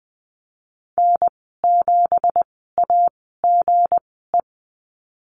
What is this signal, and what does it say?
Morse code, heard as a keyed tone, about 20 wpm, spaced as for 15 wpm, 700 Hz. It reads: N7AGE
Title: 213 Lesson 34g - Number 8 - Call Signs 20wpm